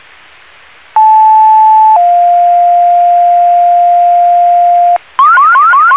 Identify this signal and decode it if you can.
Two-tone page, heard as a one-second tone, then a three-second tone, about 870 then 700 Hz, with noise. It is heard from 1.0 to 5.0 s.